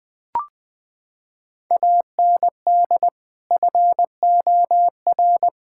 Morse code, heard as a keyed tone, about 20 wpm, 700 Hz. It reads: AND FOR
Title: Sentences with 5-Word Phrases - 20wpm